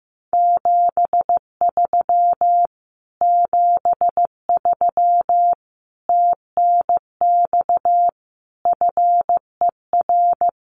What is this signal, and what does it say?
Morse code, heard as a keyed tone, about 15 wpm, 700 Hz. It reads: 73 73 TNX FER
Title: Common QSO Phrases Encore 15wpm